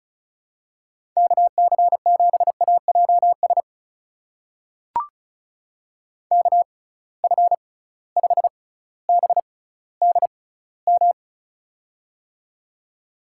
Morse code, heard as a keyed tone, about 36 wpm, spaced as for 18 wpm, 700 Hz. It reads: KC7AJS KF5BDM
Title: USA Call Signs - 6 Characters - 6x Character Spacing - 35wpm